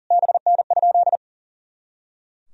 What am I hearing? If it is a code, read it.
Morse code, heard as a keyed tone, about 40 wpm, 700 Hz. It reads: BN?
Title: Sentences from Top 400 Words and Abbreviations 40wpm